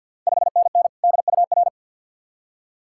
Morse code, heard as a keyed tone, about 48 wpm, 700 Hz. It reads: 5NN DUR
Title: Ontario QSO Party Exchange 50wpm